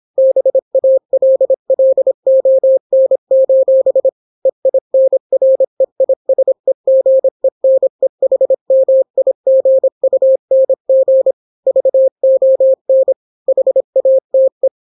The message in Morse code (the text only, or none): BALLON: EINREISEGENEHMIGUNG VON HATE